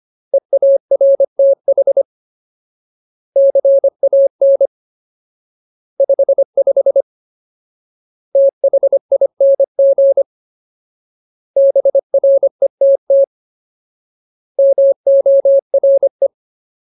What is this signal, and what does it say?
Morse code, heard as a keyed tone, about 25 wpm, 550 Hz. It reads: EARTH CAN 55 THING BRETT MORE
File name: Ultimate Word List / 25WPM Ultimate